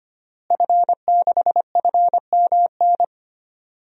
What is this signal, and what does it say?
Morse code, heard as a keyed tone, about 25 wpm, 700 Hz. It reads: F6FMN
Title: DXCC-100 Call Signs - 5 Characters - Repeated 3x - 25wpm